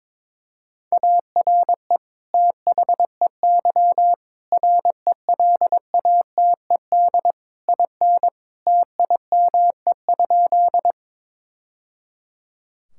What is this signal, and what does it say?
Morse code, heard as a keyed tone, about 22 wpm, 700 Hz. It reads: ARE THEY RELATED IN TIME?
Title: Sentences from Top 1000 Words - Review Part 5 - 22wpm